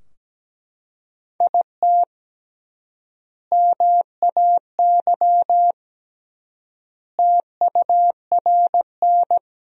Morse code, heard as a keyed tone, about 17 wpm, 700 Hz. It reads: IT MAY TURN
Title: 3x Word-Spacing - Sentences from Top 400 Words - 17wpm